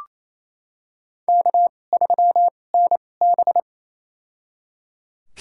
Morse code, heard as a keyed tone, about 28 wpm, spaced as for 20 wpm, 700 Hz. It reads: K3NB
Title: USA Call Signs - 4 Characters - 2x Character Spacing - 28wpm